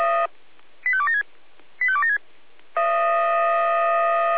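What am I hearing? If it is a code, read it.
Five-tone selective call (ZVEI), heard as quick runs of five tones, tones of about 70 ms. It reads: 85276 85276
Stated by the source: ZVEI five-tone selective call 85275 received by SDR.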